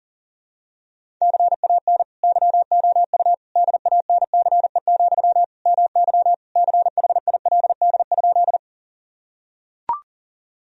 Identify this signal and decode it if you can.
Morse code, heard as a keyed tone, about 40 wpm, 700 Hz. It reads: CAN YOU DANCE, MY CHILD?
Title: Sentences from Top 1000 Words - Review Part 4 - 40wpm